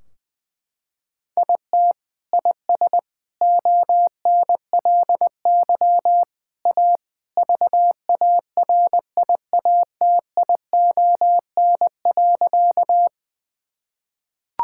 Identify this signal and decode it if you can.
Morse code, heard as a keyed tone, about 21 wpm, 700 Hz. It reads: IT IS ONLY A VARIATION.
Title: Sentences from Top 1600 Words 20wpm